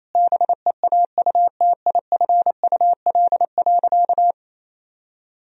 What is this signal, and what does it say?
Morse code, heard as a keyed tone, about 28 wpm, 700 Hz. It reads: BEAUTIFUL.
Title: Sentences from Top 1000 Words - Review Part 1 - 28wpm